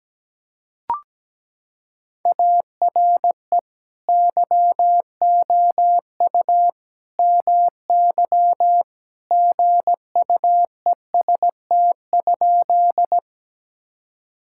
Morse code, heard as a keyed tone, about 17 wpm, 700 Hz. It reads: ARE YOU MY GUEST?
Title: Sentences from Top 1600 Words 17wpm